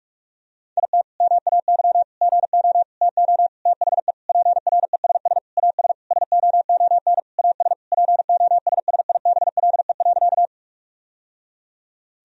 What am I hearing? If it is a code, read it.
Morse code, heard as a keyed tone, about 46 wpm, 700 Hz. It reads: IT MAY GO TO THE PRESS AS SOON AS POSSIBLE.
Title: Sentences from Top 1000 Words - Review Part 1 - 45wpm